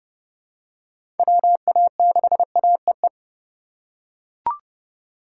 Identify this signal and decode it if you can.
Morse code, heard as a keyed tone, about 30 wpm, 700 Hz. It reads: WA6AEE
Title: K1USN SST QSO Practice - 30wpm